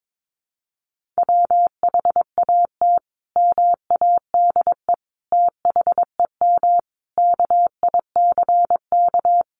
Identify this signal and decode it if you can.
Morse code, heard as a keyed tone, about 22 wpm, 700 Hz. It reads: WHAT MADE THEM KICK
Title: Sentences from Top 1300 Words 22wpm